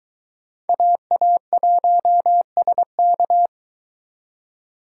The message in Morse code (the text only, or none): AA1SK